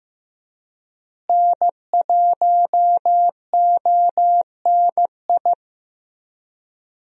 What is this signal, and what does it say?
Morse code, heard as a keyed tone, about 15 wpm, 700 Hz. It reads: N1ONI
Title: USA Call Signs - 5 Characters - Repeated 3x - 15wpm